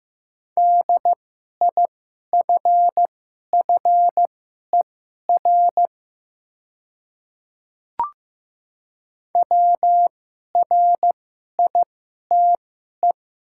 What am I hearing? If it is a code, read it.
Morse code, heard as a keyed tone, about 15 wpm, spaced as for 11 wpm, 700 Hz. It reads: DIFFER WRITE
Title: 2x Character Spacing - Top 200 Words - 15wpm